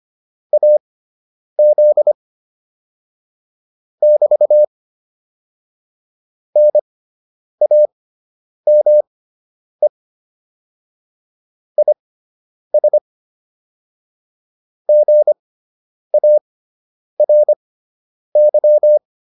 Morse code, heard as a keyed tone, about 25 wpm, spaced as for 9 wpm, 600 Hz. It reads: AZ <BT> NAME IS GARY